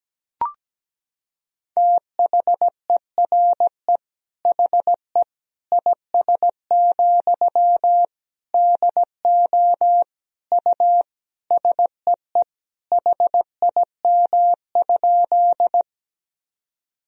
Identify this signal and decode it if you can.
Morse code, heard as a keyed tone, about 17 wpm, 700 Hz. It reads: THERE HE IS, DO U SEE HIM?
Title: Sentences from Top 100 Words and Abbreviations 17wpm